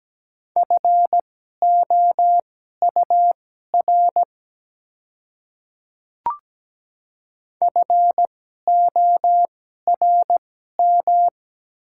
Morse code, heard as a keyed tone, about 17 wpm, spaced as for 12 wpm, 690 Hz. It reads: FOUR FORM
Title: 2x Character Spacing - Top 300 Words - 17wpm